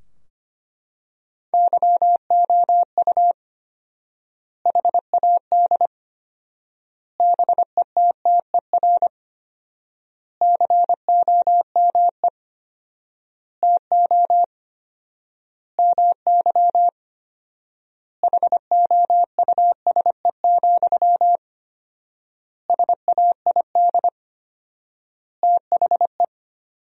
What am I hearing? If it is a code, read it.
Morse code, heard as a keyed tone, about 25 wpm, 700 Hz. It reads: YOU HAD BETTER COME TO MY HOUSE, SAID THE
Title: 4x Word-Spacing - Sentences from Top 400 Words - 25wpm